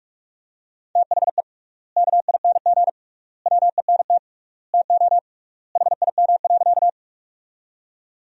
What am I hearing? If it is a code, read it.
Morse code, heard as a keyed tone, about 46 wpm, 700 Hz. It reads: THE KING WENT TO HIM.
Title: 3x Word-Spacing - Sentences from Top 400 Words - 45wpm